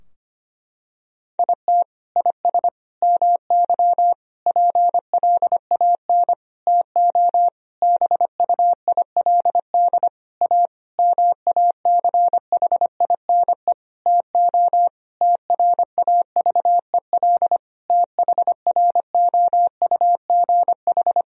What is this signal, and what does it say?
Morse code, heard as a keyed tone, about 25 wpm, 700 Hz. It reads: IT IS MY PLAN TO BUILD A MACHINE TO TRAVEL THROUGH
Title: Sentences from Top 1000 Words - Review Part 2 - 25wpm